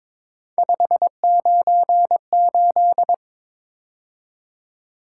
Morse code, heard as a keyed tone, about 22 wpm, 700 Hz. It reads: 598
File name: practice / 22wpm / Three Letter-Number Combinations - 22wpm